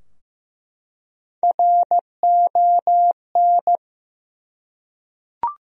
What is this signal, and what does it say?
Morse code, heard as a keyed tone, about 15 wpm, 700 Hz. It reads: RON
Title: USA Common Names - Max 4 Letters - 15wpm